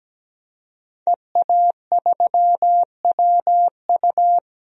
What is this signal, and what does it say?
Morse code, heard as a keyed tone, about 17 wpm, 700 Hz. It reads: EA3WU
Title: DXCC-100 Call Signs - 5 Characters - 17wpm